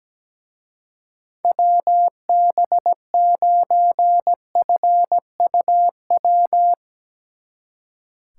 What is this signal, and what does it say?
Morse code, heard as a keyed tone, about 19 wpm, 700 Hz. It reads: WB9FUW
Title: Speed Racing - USA 2x3 Call Signs 17wpm